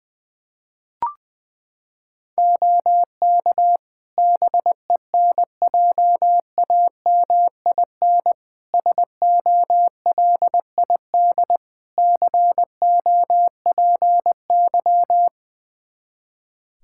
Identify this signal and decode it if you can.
Morse code, heard as a keyed tone, about 21 wpm, 700 Hz. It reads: OK BENJAMIN SOLID COPY
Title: Practice QSO - Exchange 3 - More Info - 20wpm